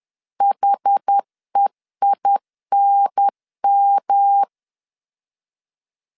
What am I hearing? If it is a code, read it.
Morse code, heard as a keyed tone, about 10 wpm, 800 Hz. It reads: HEINM
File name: CW / DET / 1 3 1500C S B 10wpm 3335